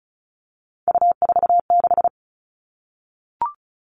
Morse code, heard as a keyed tone, about 34 wpm, 700 Hz. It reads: U46
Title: All Cut Numbers - 3-Digit 35wpm